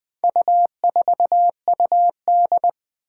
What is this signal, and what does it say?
Morse code, heard as a keyed tone, about 21 wpm, 700 Hz. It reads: U4UD
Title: Four Challenging Letter-Number Combinations 20wpm